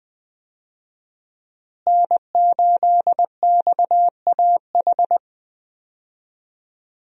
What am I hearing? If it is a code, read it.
Morse code, heard as a keyed tone, about 20 wpm, 700 Hz. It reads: N8XAH
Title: USA Call Signs - 5 Characters - Repeated 3x - 20wpm